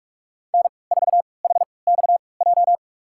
Morse code, heard as a keyed tone, about 46 wpm, spaced as for 26 wpm, 700 Hz. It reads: N4HXJ
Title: USA Call Signs - 5 Characters - 3x Character Spacing - 45wpm